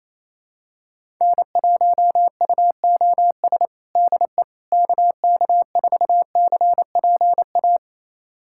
Morse code, heard as a keyed tone, about 28 wpm, 700 Hz. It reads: N1UOS DE KK4CPA
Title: USA Callsign DE Callsign 28wpm